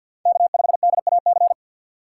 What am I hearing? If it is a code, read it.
Morse code, heard as a keyed tone, about 53 wpm, 700 Hz. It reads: K5DAC
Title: USA Call Signs - 5 Characters - 3x Character Spacing - 50wpm